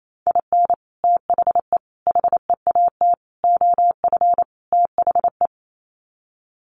Morse code, heard as a keyed tone, about 28 wpm, 700 Hz. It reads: IN THE HEAT OF THE
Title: Sets of 5 Words Encore2 28wpm